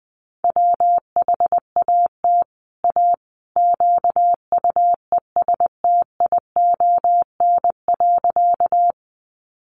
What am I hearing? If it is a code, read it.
Morse code, heard as a keyed tone, about 20 wpm, 700 Hz. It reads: WHAT A QUESTION.
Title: Sentences from Top 400 Words 20wpm